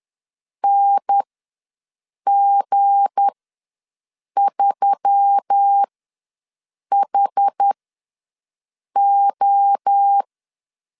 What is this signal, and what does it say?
Morse code, heard as a keyed tone, about 10 wpm, spaced as for 6 wpm, 800 Hz. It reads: NG3HO
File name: LEZIONI / 7 GF9Z / 7 GF ws 10wpm 3918